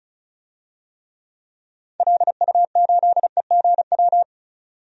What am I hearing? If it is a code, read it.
Morse code, heard as a keyed tone, about 36 wpm, 700 Hz. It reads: LU8EGW
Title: DXCC-100 Call Signs - 6 Characters - 35wpm